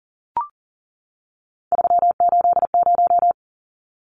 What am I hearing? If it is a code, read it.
Morse code, heard as a keyed tone, about 40 wpm, 700 Hz. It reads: 380
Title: Numbers - 3-Digit 40wpm